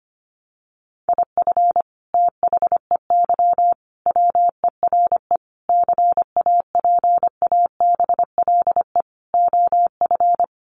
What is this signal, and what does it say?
Morse code, heard as a keyed tone, about 25 wpm, 700 Hz. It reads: IF THEY WERE CAPABLE OF
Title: Sentences from Top 1200 Words 25wpm